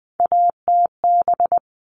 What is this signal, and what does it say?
Morse code, heard as a keyed tone, about 20 wpm, 700 Hz. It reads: ATB
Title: All Cut Numbers - 3-Digit 20wpm